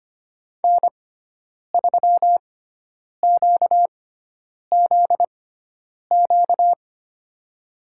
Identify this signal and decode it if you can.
Morse code, heard as a keyed tone, about 25 wpm, spaced as for 9 wpm, 700 Hz. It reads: N3QZQ
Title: USA Call Signs - 5 Characters - 6x Character Spacing - 25wpm